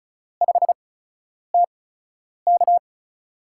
Morse code, heard as a keyed tone, about 34 wpm, spaced as for 10 wpm, 700 Hz. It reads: 5TK
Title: USA Call Signs - 4 Characters - 8x Character Spacing - 35wpm